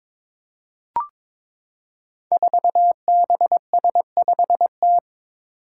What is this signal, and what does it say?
Morse code, heard as a keyed tone, about 22 wpm, 700 Hz. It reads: 4BS5T
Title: Five Challenging Letter-Number Combinations 22wpm